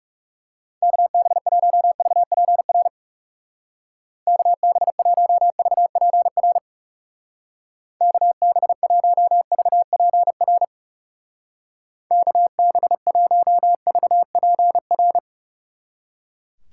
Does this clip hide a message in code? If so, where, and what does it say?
Yes: Morse code from 0.8 to 15.2 s, heard as a keyed tone, about 38 wpm, 700 Hz. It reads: KB1VPR KB1VPR KB1VPR KB1VPR